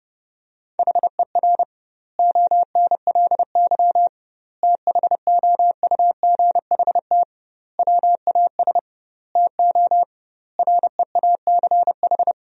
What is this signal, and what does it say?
Morse code, heard as a keyed tone, about 32 wpm, 700 Hz. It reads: HER ONLY THOUGHT WAS TO REACH
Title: 2x Word-Spacing - Sentences from Top 400 Words - 30wpm